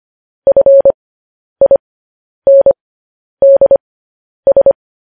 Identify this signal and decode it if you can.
Morse code, heard as a keyed tone, about 25 wpm, spaced as for 10 wpm, 550 Hz. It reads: FINDS